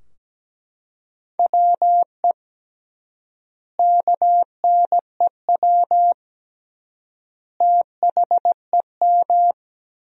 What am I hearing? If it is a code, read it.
Morse code, heard as a keyed tone, about 17 wpm, 700 Hz. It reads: WE KNEW THEM
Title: 3x Word-Spacing - Sentences from Top 400 Words - 17wpm